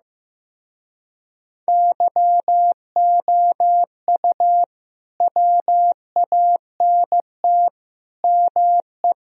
Morse code, heard as a keyed tone, about 16 wpm, 700 Hz. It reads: YOU WANT ME
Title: Speed-Racing - Sets of 4 Words 15wpm